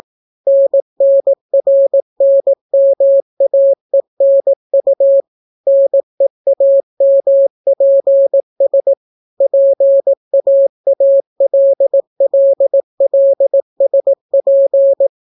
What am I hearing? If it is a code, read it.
Morse code, heard as a keyed tone, about 18 wpm, 550 Hz. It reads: NNRNMAENU NEAMPS PAALLLSP